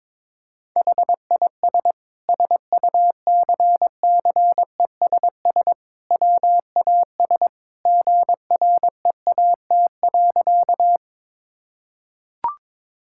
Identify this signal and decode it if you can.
Morse code, heard as a keyed tone, about 22 wpm, 700 Hz. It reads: HIS SUCCESS WAS GREAT.
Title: Sentences from Top 1000 Words - Review Part 6 - 22wpm